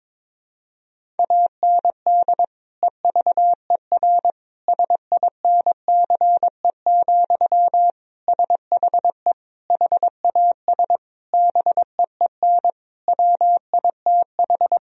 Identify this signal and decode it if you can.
Morse code, heard as a keyed tone, about 22 wpm, 700 Hz. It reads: AND EVER SINCE, SHE HAS BEEN WITH
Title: Sentences from Top 1000 Words - Review Part 1 - 22wpm